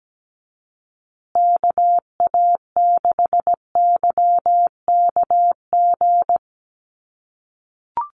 Morse code, heard as a keyed tone, about 17 wpm, 700 Hz. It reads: KA6YKG
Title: USA Call Signs Repeated 2x 17wpm